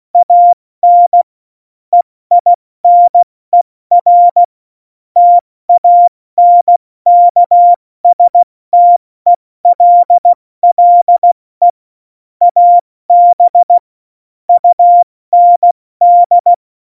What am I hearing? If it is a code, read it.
Morse code, heard as a keyed tone, about 16 wpm, 700 Hz. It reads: AN EINER TANKSTELLE AB UND
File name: Morsekurs / Witze / Witz 11 August 2020, 70 F80